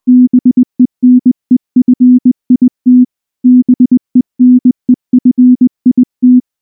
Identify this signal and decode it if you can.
Morse code, heard as a keyed tone, about 20 wpm, 250 Hz. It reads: BENEFIT BENEFIT
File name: morse / morse-training-20WPM-9